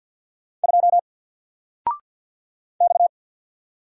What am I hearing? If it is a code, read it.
Morse code, heard as a keyed tone, about 50 wpm, 700 Hz. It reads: 2 X
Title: Single Letter-Number 50wpm